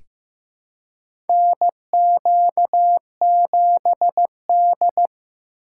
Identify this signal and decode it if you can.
Morse code, heard as a keyed tone, about 15 wpm, 700 Hz. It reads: NQ7D
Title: USA Call Signs - 4 Characters - 8x Character Spacing - 15wpm